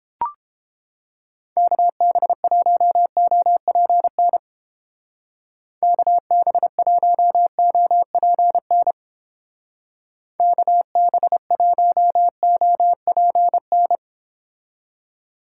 Morse code, heard as a keyed tone, about 30 wpm, 700 Hz. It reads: KB1OPN KB1OPN KB1OPN